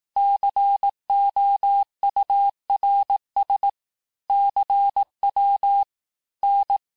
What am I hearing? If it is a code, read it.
Morse code, heard as a keyed tone, about 18 wpm, 800 Hz. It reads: COURS CW N
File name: cours-du-lundi files / cours-1852-texte-chiffre-vitesse-17MPM